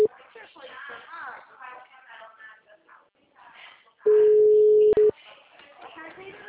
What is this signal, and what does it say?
Ringback tone (Europe, 425 Hz), with speech.
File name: out- -2010- - - .18615